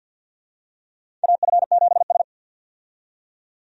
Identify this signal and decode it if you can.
Morse code, heard as a keyed tone, about 52 wpm, 700 Hz. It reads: AF7S
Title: USA Call Signs - 4 Characters - 50wpm